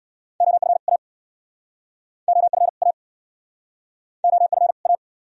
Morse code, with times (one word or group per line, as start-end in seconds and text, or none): 0.4-0.8 OF
0.9-1.0 A
2.3-2.7 OF
2.8-2.9 A
4.2-4.7 OF
4.9-4.9 A